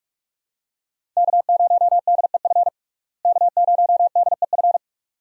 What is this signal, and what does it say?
Morse code, heard as a keyed tone, about 46 wpm, 700 Hz. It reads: K0DEF K0DEF
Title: USA Call Signs - 5 Characters - Repeated 2x - 45wpm